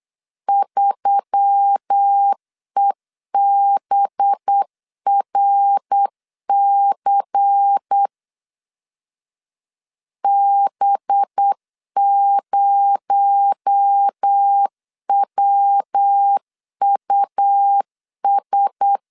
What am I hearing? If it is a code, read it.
Morse code, heard as a keyed tone, about 8 wpm, 800 Hz. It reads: 3EBRC B0WUS
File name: CW / DET / 1 6 1500C S B 8wpm 5142